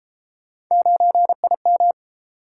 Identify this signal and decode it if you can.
Morse code, heard as a keyed tone, about 30 wpm, 700 Hz. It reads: 9IM